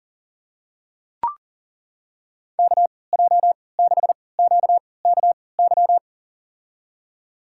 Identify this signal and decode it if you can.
Morse code, heard as a keyed tone, about 40 wpm, spaced as for 23 wpm, 700 Hz. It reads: KJ6QKY
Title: USA Call Signs - 6 Characters - 3x Character Spacing - 40wpm